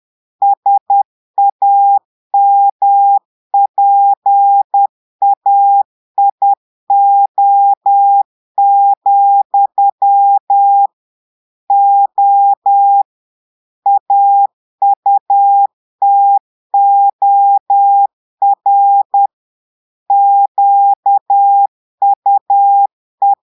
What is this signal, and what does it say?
Morse code, heard as a keyed tone, about 10 wpm, 800 Hz. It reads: SAMPAIO, O AUTOR QUE
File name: Portugais - texte en clair 1 - vitesse - 10 mots minute